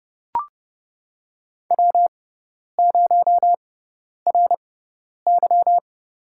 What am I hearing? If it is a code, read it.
Morse code, heard as a keyed tone, about 30 wpm, spaced as for 10 wpm, 700 Hz. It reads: W0RY
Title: USA Call Signs - 4 Characters - 6x Character Spacing - 30wpm